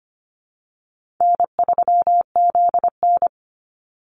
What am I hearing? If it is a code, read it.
Morse code, heard as a keyed tone, about 25 wpm, 700 Hz. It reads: N3ZN
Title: Easy Call Signs Repeated 3x - 25wpm